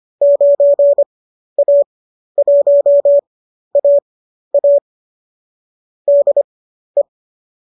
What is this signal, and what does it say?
Morse code, heard as a keyed tone, about 25 wpm, spaced as for 12 wpm, 590 Hz. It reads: 9A1AA DE
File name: qso012 12